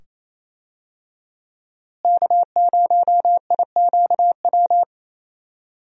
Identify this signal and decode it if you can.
Morse code, heard as a keyed tone, about 28 wpm, 700 Hz. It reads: K0IQW
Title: USA Call Signs - 5 Characters - Repeated 2x - 28wpm